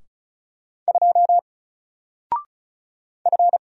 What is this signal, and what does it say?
Morse code, heard as a keyed tone, about 36 wpm, 700 Hz. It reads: 2 F
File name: Single-Letter-Number-35wpm